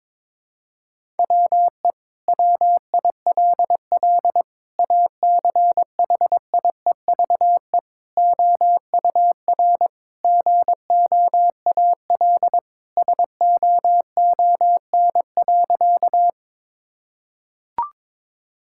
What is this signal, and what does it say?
Morse code, heard as a keyed tone, about 21 wpm, 700 Hz. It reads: WE WILL ACHIEVE OUR GOAL SOON.
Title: Sentences from Top 1000 Words - Review Part 6 - 22wpm